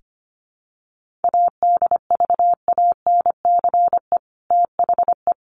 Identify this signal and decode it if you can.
Morse code, heard as a keyed tone, about 25 wpm, 700 Hz. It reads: ADVANCE THE
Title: Sentences from Top 1400 Words 25wpm